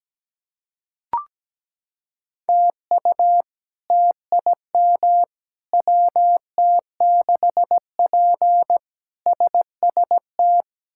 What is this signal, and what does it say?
Morse code, heard as a keyed tone, about 17 wpm, 700 Hz. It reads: TU TIM WT6P SST
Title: K1USN SST QSO Practice - 17wpm